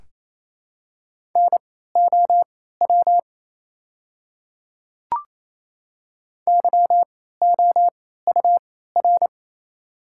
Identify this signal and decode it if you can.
Morse code, heard as a keyed tone, about 28 wpm, spaced as for 16 wpm, 700 Hz. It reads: NOW YOUR
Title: 3x Character Spacing - Top 100 Words - 28wpm